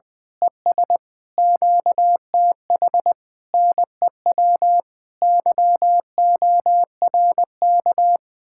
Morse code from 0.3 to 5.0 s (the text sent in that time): ES QTH NEW